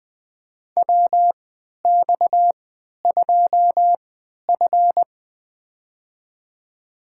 Morse code, heard as a keyed tone, about 20 wpm, spaced as for 11 wpm, 700 Hz. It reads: WX2F
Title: USA Call Signs - 4 Characters - 3x Character Spacing - 20wpm